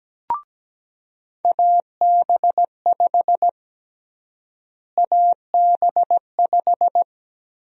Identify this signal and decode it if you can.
Morse code, heard as a keyed tone, about 17 wpm, 700 Hz. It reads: AB5 AB5
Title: USA Call Sign Prefixes - Repeated 2x - 17wpm